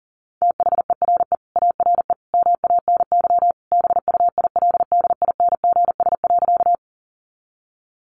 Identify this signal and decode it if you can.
Morse code, heard as a keyed tone, about 40 wpm, 700 Hz. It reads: THERE ARE MANY BUILDINGS.
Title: Easy Phrases 40wpm